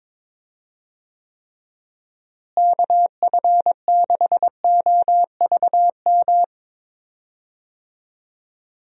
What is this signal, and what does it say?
Morse code, heard as a keyed tone, about 22 wpm, 700 Hz. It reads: KF6OVM
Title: USA Call Signs - 6 Characters - Repeated 2x - 22wpm